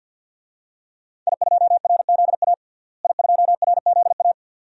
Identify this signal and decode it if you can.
Morse code, heard as a keyed tone, about 52 wpm, 700 Hz. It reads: I2RZA I2RZA
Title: DXCC-100 Call Signs - 5 Characters - Repeated 2x - 50wpm